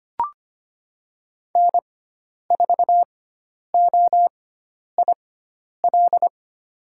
Morse code, heard as a keyed tone, about 25 wpm, spaced as for 10 wpm, 700 Hz. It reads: N4OIL
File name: Lesson 16g - Letter c - Call Signs 25wpm